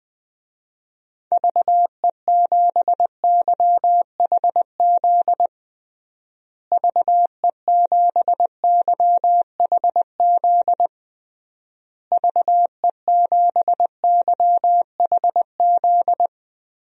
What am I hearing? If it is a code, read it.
Morse code, heard as a keyed tone, about 20 wpm, 700 Hz. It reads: VE7YHZ VE7YHZ VE7YHZ
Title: DXCC-100 Call Signs - 6 Characters - Repeated 3x - 20wpm